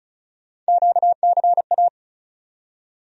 Morse code, heard as a keyed tone, about 36 wpm, 700 Hz. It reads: QCA
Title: USA Call Sign Postfixes - 3 Letters 35wpm